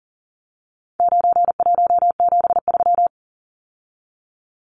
Morse code, heard as a keyed tone, about 39 wpm, 700 Hz. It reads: 9173
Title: Numbers - 4-Digit 40wpm